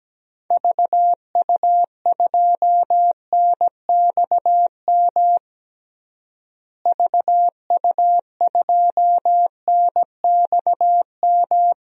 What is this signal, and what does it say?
Morse code, heard as a keyed tone, about 17 wpm, 700 Hz. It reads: VU2NXM VU2NXM
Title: DXCC-100 Call Signs - 6 Characters - Repeated 3x - 17wpm